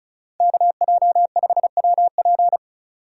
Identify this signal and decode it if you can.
Morse code, heard as a keyed tone, about 34 wpm, 700 Hz. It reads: KJ5WP
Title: USA Call Signs - 5 Characters - 3x Character Spacing - 35wpm